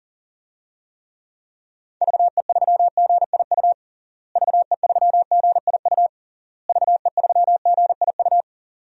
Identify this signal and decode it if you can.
Morse code, heard as a keyed tone, about 40 wpm, 700 Hz. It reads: VE3GIU VE3GIU VE3GIU
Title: DXCC-100 Call Signs - 6 Characters - Repeated 3x - 40wpm